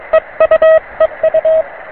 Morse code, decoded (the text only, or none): EU EU